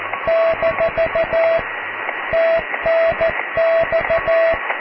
Morse code, heard as a keyed tone, about 14 wpm, 660 Hz, with radio noise. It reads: - TNX
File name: cw / QSO+QRM